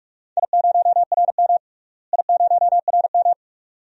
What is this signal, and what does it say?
Morse code, heard as a keyed tone, about 41 wpm, 700 Hz. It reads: I0RM I0RM
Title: DXCC-100 Call Signs - 4 Characters - Repeated 2x - 45wpm